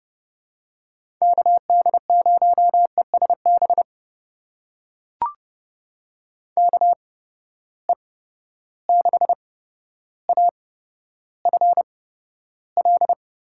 Morse code, heard as a keyed tone, about 30 wpm, spaced as for 13 wpm, 700 Hz. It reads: KD0ESB KE6AFL
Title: USA Call Signs - 6 Characters - 8x Character Spacing - 30wpm